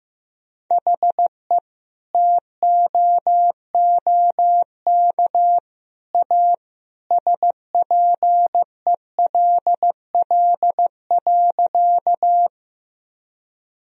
Morse code, heard as a keyed tone, about 16 wpm, 700 Hz. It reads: HE TOOK A SPELL.
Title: Sentences from Top 1000 Words - Review Part 1 - 15wpm